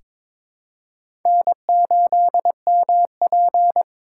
Morse code, heard as a keyed tone, about 22 wpm, 700 Hz. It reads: N8MP
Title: USA Call Signs - 4 Characters - 3x Character Spacing - 22wpm